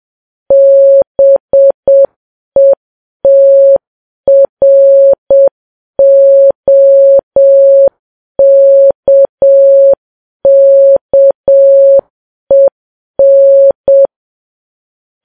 Morse code, heard as a keyed tone, about 7 wpm, 550 Hz. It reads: BETROKKEN
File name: Files Web tls / Morse on9fv / morse7kps